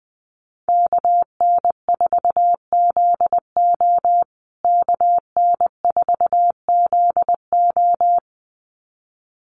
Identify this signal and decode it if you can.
Morse code, heard as a keyed tone, about 20 wpm, 700 Hz. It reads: KN4ZO KN4ZO
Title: USA Call Signs Repeated 2x 20wpm